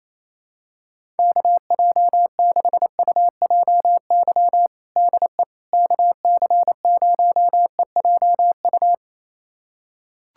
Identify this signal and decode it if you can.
Morse code, heard as a keyed tone, about 27 wpm, 700 Hz. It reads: KJ6UJY DE KC0EJU
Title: USA Callsign DE Callsign - 6 Characters - 28wpm